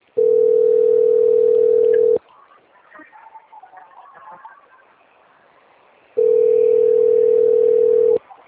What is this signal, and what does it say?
Ringback tone (North America), 440 + 480 Hz.
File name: out- - - - - .7766